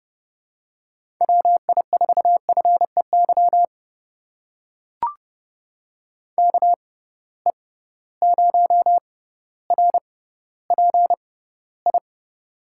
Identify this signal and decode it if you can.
Morse code, heard as a keyed tone, about 29 wpm, spaced as for 15 wpm, 700 Hz. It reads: WI4FEY KE0RPI
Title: USA Call Signs - 6 Characters - 6x Character Spacing - 30wpm